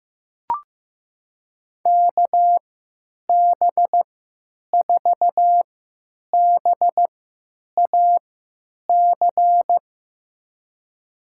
Morse code, heard as a keyed tone, about 15 wpm, spaced as for 9 wpm, 700 Hz. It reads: KB4BAC
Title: USA Call Signs - 6 Characters - 3x Character Spacing - 15wpm